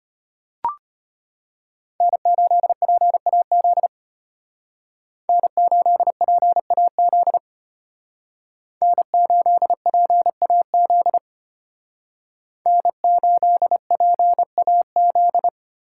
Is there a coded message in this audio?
Yes: Morse code from 2.0 to 15.5 s, heard as a keyed tone, about 33 wpm, 700 Hz. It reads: N8PAZ N8PAZ N8PAZ N8PAZ